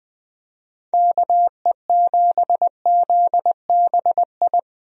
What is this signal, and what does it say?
Morse code, heard as a keyed tone, about 20 wpm, 700 Hz. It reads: KE7ZBI
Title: USA Call Signs - 6 Characters - Repeated 2x - 20wpm